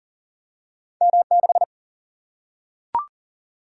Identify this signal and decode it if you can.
Morse code, heard as a keyed tone, about 39 wpm, 700 Hz. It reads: M6
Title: Single-Letter Single-Number 40wpm